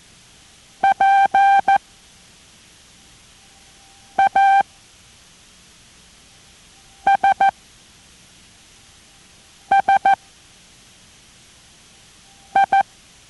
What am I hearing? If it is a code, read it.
Morse code, heard as a keyed tone, about 14 wpm, spaced as for 3 wpm, 790 Hz. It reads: PASSI